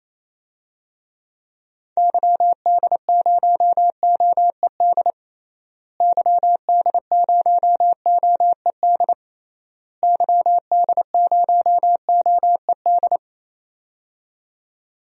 Morse code, heard as a keyed tone, about 27 wpm, 700 Hz. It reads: YD0OED YD0OED YD0OED
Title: DXCC-100 Call Signs - 6 Characters - Repeated 3x - 28wpm